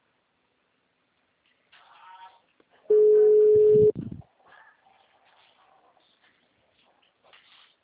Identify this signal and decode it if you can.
Ringback tone (Europe, 425 Hz).